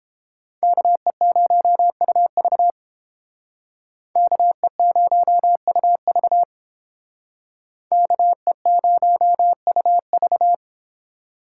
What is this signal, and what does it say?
Morse code, heard as a keyed tone, about 29 wpm, 700 Hz. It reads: KE0UV KE0UV KE0UV